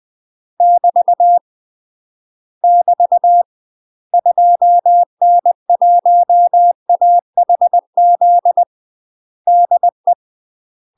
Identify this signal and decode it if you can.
Morse code, heard as a keyed tone, about 20 wpm, 700 Hz. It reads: <BT> <BT> 2N1AHZ DE
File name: UK MASTER QSOs TRAINING / 20WPM / UK MASTER QSOs 0046